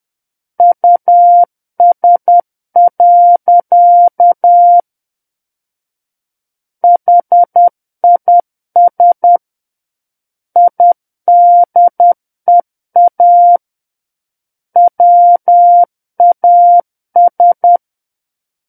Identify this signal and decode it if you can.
Morse code, heard as a keyed tone, about 10 wpm, 700 Hz. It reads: US. HIS IDEA WAS
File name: Book One / 01-Chapter 1 - 10 WPM